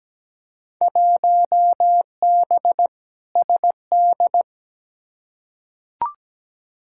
Morse code, heard as a keyed tone, about 17 wpm, 700 Hz. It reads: 1B SD
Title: Field Day Exchanges Encore - 17wpm